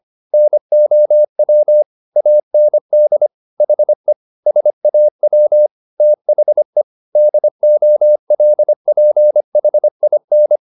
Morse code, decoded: NOW AND HE SAW THE DOLPHIN

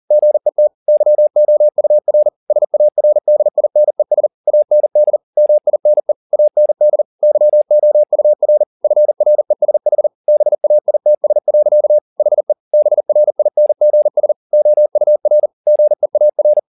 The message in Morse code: GET YOUR SARDINES AND MINE AND YOUR FRESH BAITS. HE BRINGS OUR GEAR

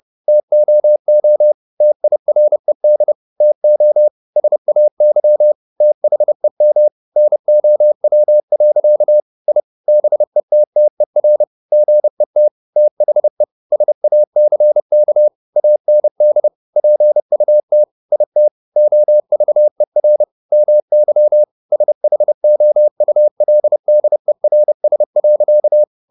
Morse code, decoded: TOO TIRED TO SAY THEM NOW. I BETTER GET THE SACK AND PUT IT OVER MY SHOULDERS.

